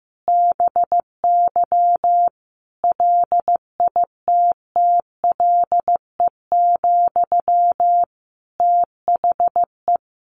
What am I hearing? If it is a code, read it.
Morse code, heard as a keyed tone, about 15 wpm, 700 Hz. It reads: BY LITTLE, THE
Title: Sentences from Top 1500 Words 15wpm